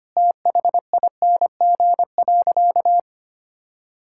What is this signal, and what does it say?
Morse code, heard as a keyed tone, about 25 wpm, 700 Hz. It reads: THING.
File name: practice / 25wpm / Sentences with 3-Word Phrases - 25wpm